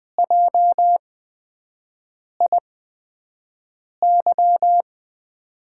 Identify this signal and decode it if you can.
Morse code, heard as a keyed tone, about 20 wpm, spaced as for 5 wpm, 700 Hz. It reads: JIY
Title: USA Call Signs - 6 Characters - 8x Character Spacing - 20wpm